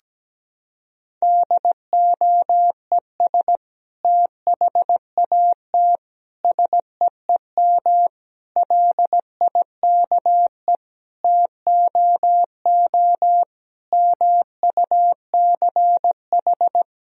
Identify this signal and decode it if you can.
Morse code, heard as a keyed tone, about 17 wpm, 700 Hz. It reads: DOES THAT SEEM LIKE TOO MUCH
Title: Sentences from Top 1000 Words - Review Part 1 - 17wpm